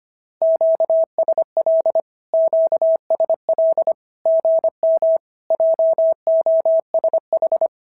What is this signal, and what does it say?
Morse code, heard as a keyed tone, about 25 wpm, 650 Hz. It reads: QSL QSL GM JOSH
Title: SOTA QSO Practice 25wpm